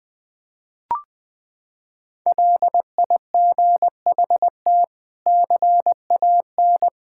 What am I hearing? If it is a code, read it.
Morse code, heard as a keyed tone, about 20 wpm, 700 Hz. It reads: LIGHT CAN
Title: Sentences from Top 1000 Words - Review Part 6 - 20wpm